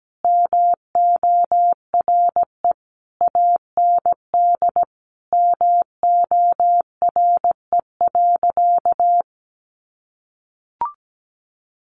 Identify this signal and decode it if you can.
Morse code, heard as a keyed tone, about 17 wpm, 700 Hz. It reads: MORE AND MORE.
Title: Sentences from Top 500 Words 17wpm